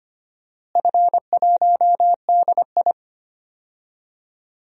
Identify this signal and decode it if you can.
Morse code, heard as a keyed tone, about 25 wpm, 700 Hz. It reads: F1DI
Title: DXCC-100 Call Signs - 4 Characters - Repeated 3x - 25wpm